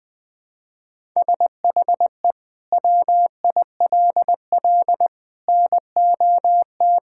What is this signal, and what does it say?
Morse code, heard as a keyed tone, about 20 wpm, 700 Hz. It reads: SHE WILL NOT
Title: Sentences from Top 1000 Words - Review Part 6 - 20wpm